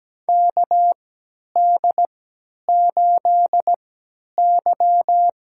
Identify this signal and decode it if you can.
Morse code, heard as a keyed tone, about 17 wpm, spaced as for 10 wpm, 700 Hz. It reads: KD8Y
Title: USA Call Signs - 6 Characters - 3x Character Spacing - 17wpm